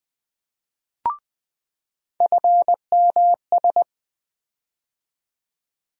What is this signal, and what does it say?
Morse code, heard as a keyed tone, about 20 wpm, 700 Hz. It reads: FMS
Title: USA Call Sign Postfixes - 3 Letters 20wpm